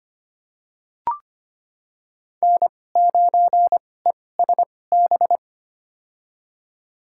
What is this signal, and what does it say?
Morse code, heard as a keyed tone, about 25 wpm, spaced as for 18 wpm, 700 Hz. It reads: N9ESB
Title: USA Call Signs - 5 Characters - 2x Character Spacing - 25wpm